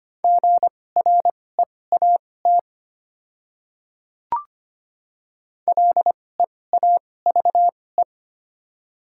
Morse code, heard as a keyed tone, about 25 wpm, spaced as for 18 wpm, 700 Hz. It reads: GREAT LEAVE
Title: 2x Character Spacing - Top 500 Words - 25wpm